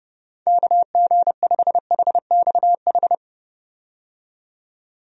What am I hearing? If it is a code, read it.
Morse code, heard as a keyed tone, about 31 wpm, 700 Hz. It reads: KG5HXH
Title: USA Call Signs - 6 Characters - 30wpm